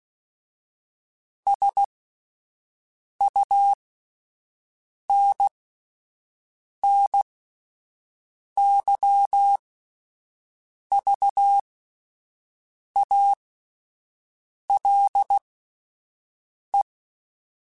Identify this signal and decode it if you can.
Morse code, heard as a keyed tone, about 16 wpm, spaced as for 5 wpm, 800 Hz. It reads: SUNNYVALE